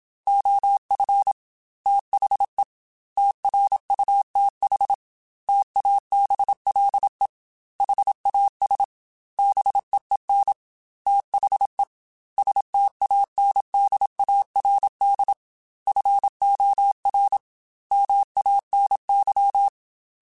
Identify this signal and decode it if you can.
Morse code, heard as a keyed tone, about 27 wpm, 800 Hz. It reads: OF THE TRUTH TABLE HAS BEEN THE STANDARD FOR MANY
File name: Technical Subjects / Technical Subjects-mp3s / AND Gate @25